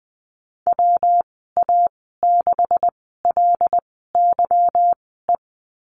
Morse code, heard as a keyed tone, about 20 wpm, spaced as for 15 wpm, 700 Hz. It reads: WA6LYE
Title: 213 Lesson 34g - Number 8 - Call Signs 20wpm